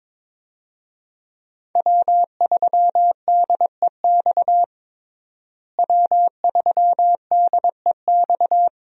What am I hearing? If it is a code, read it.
Morse code, heard as a keyed tone, about 22 wpm, 700 Hz. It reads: W3DEX W3DEX
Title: USA Call Signs - 5 Characters - Repeated 2x - 22wpm